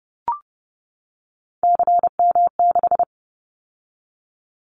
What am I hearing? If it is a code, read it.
Morse code, heard as a keyed tone, about 30 wpm, 700 Hz. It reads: CM6
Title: Callsign Prefixes for Top 100 DXCC - 30wpm